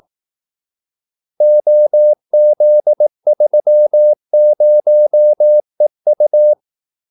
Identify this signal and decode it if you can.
Morse code, heard as a keyed tone, about 18 wpm, 600 Hz. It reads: OZ30EU
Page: Kaldesignaler 18 wpm